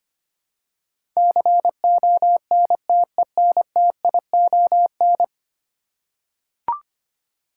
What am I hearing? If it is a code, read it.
Morse code, heard as a keyed tone, about 25 wpm, 700 Hz. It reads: CONTENTION
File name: 10 Letter Words - First 3 Letters Given - 25wpm